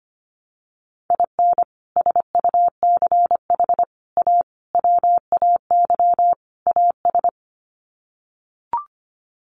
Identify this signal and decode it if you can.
Morse code, heard as a keyed tone, about 25 wpm, 690 Hz. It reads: IN SUCH A WAY AS
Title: Sets of 5 Words Encore 25wpm